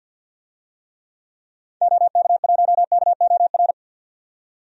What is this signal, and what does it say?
Morse code, heard as a keyed tone, about 52 wpm, 700 Hz. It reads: OK1KOR
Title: DXCC-100 Call Signs - 6 Characters - 50wpm